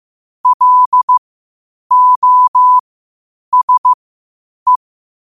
Morse code, heard as a keyed tone, about 15 wpm, spaced as for 8 wpm, 1000 Hz. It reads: LOSE
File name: qotdfiles / 8wpm